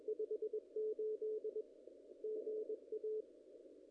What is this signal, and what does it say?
Morse code, heard as a keyed tone, about 23 wpm, 440 Hz. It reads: 58 GA
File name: contests / ss-cw-k5tr-op-n5rz -0000